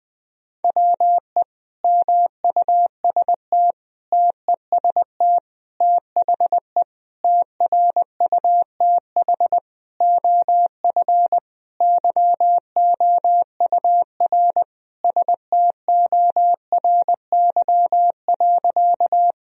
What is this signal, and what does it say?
Morse code, heard as a keyed tone, about 20 wpm, 700 Hz. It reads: WE MUST TEST THE TRUTH OF YOUR STORY.
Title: Sentences from Top 1000 Words - Review Part 3 - 20wpm